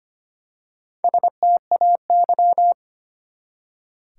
Morse code, heard as a keyed tone, about 25 wpm, 700 Hz. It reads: STAY
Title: Four Letter Words - 25wpm